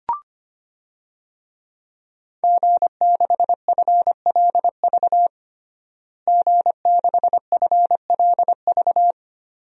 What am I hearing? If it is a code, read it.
Morse code, heard as a keyed tone, about 25 wpm, 700 Hz. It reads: G6FLV G6FLV
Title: DXCC-100 Call Signs - 5 Characters - Repeated 3x - 25wpm